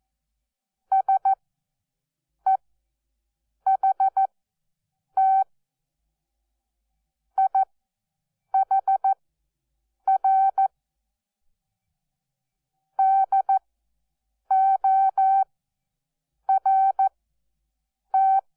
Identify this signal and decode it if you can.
Morse code, heard as a keyed tone, about 14 wpm, spaced as for 7 wpm, 770 Hz. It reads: SEHT IHR DORT
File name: Onlinelehrgang / morsen / DJ4UF Morsen12